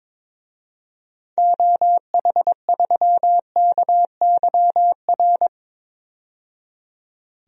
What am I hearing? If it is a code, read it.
Morse code, heard as a keyed tone, about 22 wpm, 700 Hz. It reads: OH3KYR
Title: DXCC-100 Call Signs - 6 Characters - 22wpm